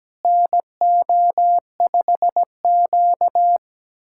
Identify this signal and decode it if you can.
Morse code, heard as a keyed tone, about 17 wpm, 700 Hz. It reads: NO5Q
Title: USA Callsign DE Callsign - 4 Characters - 17wpm